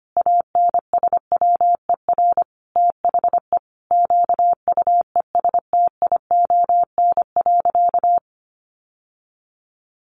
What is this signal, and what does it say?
Morse code, heard as a keyed tone, about 25 wpm, 700 Hz. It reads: ANSWER THE QUESTION.
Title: Sentences from Top 1200 Words 25wpm